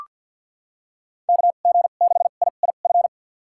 Morse code, heard as a keyed tone, about 45 wpm, spaced as for 34 wpm, 700 Hz. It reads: KG6IIF
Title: USA Call Signs - 2x Character Spacing - 50wpm